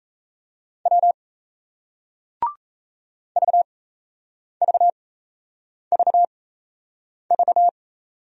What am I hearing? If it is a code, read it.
Morse code, heard as a keyed tone, about 37 wpm, spaced as for 8 wpm, 700 Hz. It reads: W VVVV